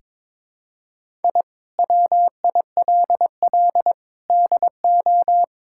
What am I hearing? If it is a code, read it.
Morse code, heard as a keyed tone, about 22 wpm, 700 Hz. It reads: I WILL DO
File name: practice / 22wpm / Sentences with 8-Word Phrases - 22wpm